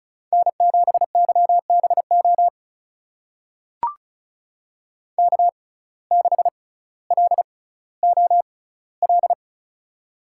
Morse code, heard as a keyed tone, about 36 wpm, spaced as for 18 wpm, 700 Hz. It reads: N7YBO K6LOL
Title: USA Call Signs - 5 Characters - 6x Character Spacing - 35wpm